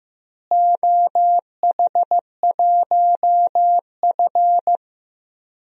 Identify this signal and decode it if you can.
Morse code, heard as a keyed tone, about 15 wpm, 700 Hz. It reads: OH1F
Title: DXCC-100 Call Signs - 4 Characters - Repeated 3x - 15wpm